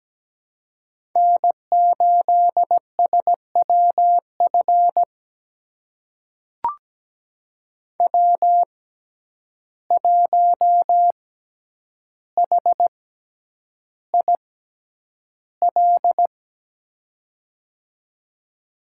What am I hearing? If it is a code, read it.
Morse code, heard as a keyed tone, about 17 wpm, spaced as for 9 wpm, 700 Hz. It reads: N8SWF W1HIL
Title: USA Call Signs - 5 Characters - 6x Character Spacing - 17wpm